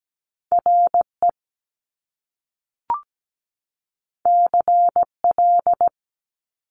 Morse code, heard as a keyed tone, about 17 wpm, 700 Hz. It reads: RE CL
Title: Top Two Letter Combinations in Words 17wpm